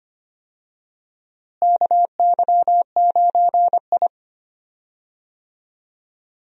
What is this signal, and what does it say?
Morse code, heard as a keyed tone, about 25 wpm, 700 Hz. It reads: KY9I
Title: USA Call Signs - 4 Characters - Repeated 3x - 25wpm